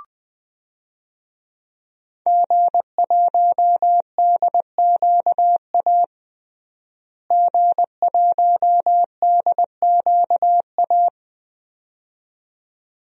Morse code, heard as a keyed tone, about 20 wpm, 700 Hz. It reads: G1DQA G1DQA
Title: DXCC-100 Call Signs - 5 Characters - Repeated 2x - 20wpm